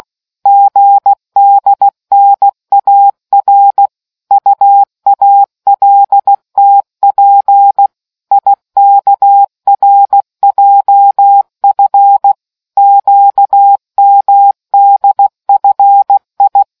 Morse code, heard as a keyed tone, about 16 wpm, 800 Hz. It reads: GDNAR UALTP IKRJF QMDFI